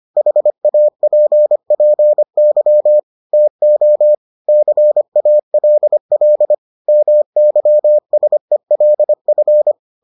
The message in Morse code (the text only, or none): HAPPY TO CALL MYSELF